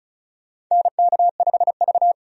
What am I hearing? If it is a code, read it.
Morse code, heard as a keyed tone, about 34 wpm, 700 Hz. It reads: NK5V
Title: USA Call Signs - 4 Characters - 35wpm